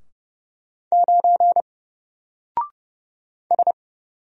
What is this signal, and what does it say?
Morse code, heard as a keyed tone, about 30 wpm, 700 Hz. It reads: 9 S
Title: Single Character (Letter:Number:Punctuation:Prosign) 30wpm